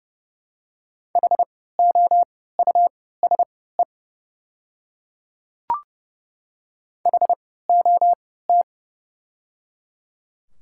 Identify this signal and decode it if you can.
Morse code, heard as a keyed tone, about 29 wpm, spaced as for 17 wpm, 700 Hz. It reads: HOUSE HOT
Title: 3x Character Spacing - Top 300 Words - 30wpm